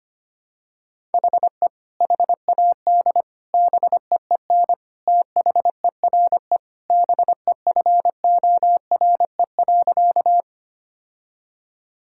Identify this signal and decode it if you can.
Morse code, heard as a keyed tone, about 25 wpm, 700 Hz. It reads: HE HAD BEEN THERE BEFORE.
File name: practice / 25wpm / Sentences with 3-Word Phrases - 25wpm